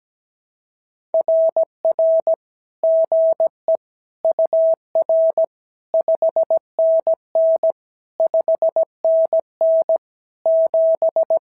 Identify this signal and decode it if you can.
Morse code, heard as a keyed tone, about 17 wpm, 650 Hz. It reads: RR GE UR 5NN 5NN 7
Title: SOTA QSO Practice 17wpm